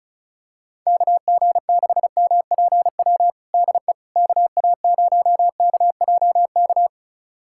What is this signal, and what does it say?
Morse code, heard as a keyed tone, about 36 wpm, 700 Hz. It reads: KG6MPW DE KA0KJK
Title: USA Callsign DE Callsign - 6 Characters - 35wpm